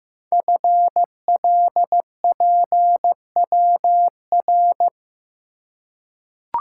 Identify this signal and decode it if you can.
Morse code, heard as a keyed tone, about 15 wpm, 700 Hz. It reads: FLPWR
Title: Code Groups - 5 Letters - 15wpm